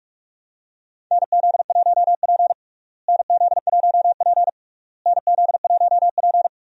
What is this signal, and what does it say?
Morse code, heard as a keyed tone, about 43 wpm, 700 Hz. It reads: NZ1P NZ1P NZ1P
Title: USA Call Signs - 4 Characters - Repeated 3x - 45wpm